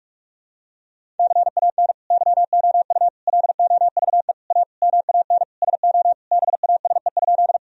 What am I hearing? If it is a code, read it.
Morse code, heard as a keyed tone, about 46 wpm, 700 Hz. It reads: CAN YOU LOVE A MAN SO BASE?
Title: Sentences from Top 1000 Words - Review Part 1 - 45wpm